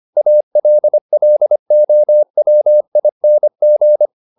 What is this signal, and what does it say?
Morse code, heard as a keyed tone, about 25 wpm, 600 Hz. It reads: ALLOWING